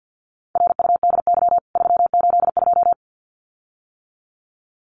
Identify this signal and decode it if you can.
Morse code, heard as a keyed tone, about 50 wpm, 700 Hz. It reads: RUDY 382
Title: CWT Exchange 50wpm